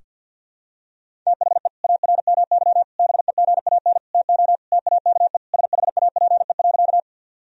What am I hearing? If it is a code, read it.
Morse code, heard as a keyed tone, about 50 wpm, 700 Hz. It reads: THE ARMY BEGAN TO TAKE SHAPE.
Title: Sentences from Top 1000 Words - Review Part 3 - 50wpm